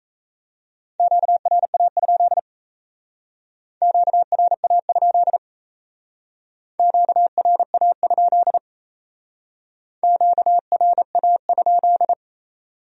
Morse code, heard as a keyed tone, about 37 wpm, 700 Hz. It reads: QRA? QRA? QRA? QRA?